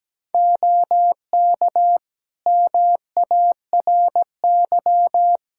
Morse code, heard as a keyed tone, about 17 wpm, 700 Hz. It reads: OK MARY
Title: Practice QSO - Exchange 3 - More Info - 17wpm